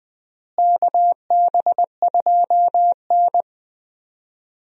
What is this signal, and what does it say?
Morse code, heard as a keyed tone, about 21 wpm, 700 Hz. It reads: KB2N
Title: USA Call Signs - 4 Characters - Repeated 3x - 20wpm